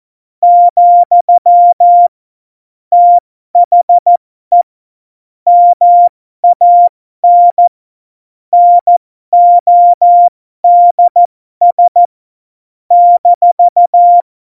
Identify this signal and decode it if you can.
Morse code, heard as a keyed tone, about 14 wpm, spaced as for 12 wpm, 700 Hz. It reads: , THE MAN NODS -
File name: Morsekurs / Witze